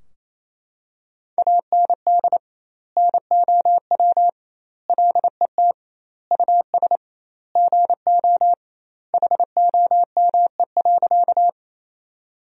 Morse code, heard as a keyed tone, about 28 wpm, 700 Hz. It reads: AND NOW LET US GO HOME.
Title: 2x Word-Spacing - Sentences from Top 300 Words - 28wpm